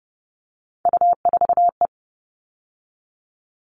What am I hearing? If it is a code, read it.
Morse code, heard as a keyed tone, about 30 wpm, 700 Hz. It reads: U4E
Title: All Cut Numbers - 3-Digit 30wpm